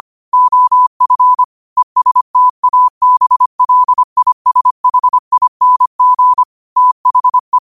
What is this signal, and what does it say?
Morse code, heard as a keyed tone, about 25 wpm, 1000 Hz. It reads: OF ESTABLISHING THE
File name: qotdfiles / 25wpm